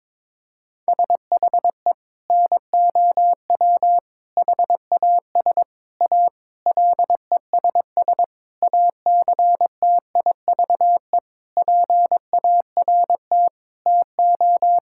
Morse code, heard as a keyed tone, about 22 wpm, 700 Hz. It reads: SHE NOW HAS A LESS ACTIVE PART TO
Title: Sentences from Top 1000 Words - Review Part 6 - 22wpm